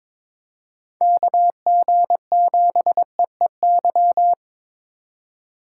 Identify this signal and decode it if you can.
Morse code, heard as a keyed tone, about 22 wpm, 700 Hz. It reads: KG7EEY
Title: USA Call Signs - 6 Characters - 2x Character Spacing - 22wpm